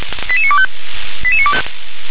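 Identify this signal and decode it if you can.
Five-tone selective call (ZVEI), heard as quick runs of five tones, tones of about 70 ms. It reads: 88925
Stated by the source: Five-tone ZVEI selective call 88953 received by SDR.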